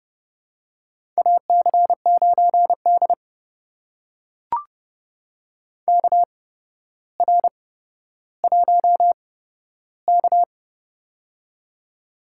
Morse code, heard as a keyed tone, about 30 wpm, spaced as for 13 wpm, 700 Hz. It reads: AC9D KR1K
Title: USA Call Signs - 4 Characters - 8x Character Spacing - 30wpm